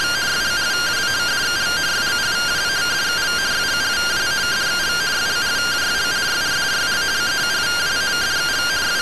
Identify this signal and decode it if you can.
Two-tone teleprinter signal, about 170 Hz shift.